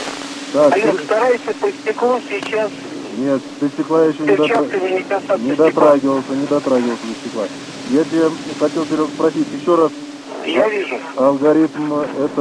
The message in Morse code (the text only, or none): none